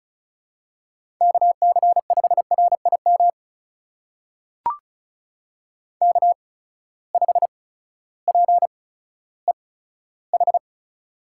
Morse code, heard as a keyed tone, about 34 wpm, spaced as for 16 wpm, 700 Hz. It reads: KC5RIM K5PEH
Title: USA Call Signs - 8x Character Spacing - 35wpm